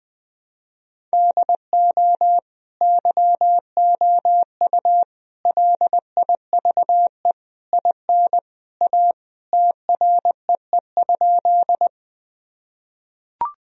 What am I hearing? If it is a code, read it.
Morse code, heard as a keyed tone, about 21 wpm, 700 Hz. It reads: DO YOU LIVE IN A TREE?
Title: Sentences from Top 1000 Words - Review Part 1 - 20wpm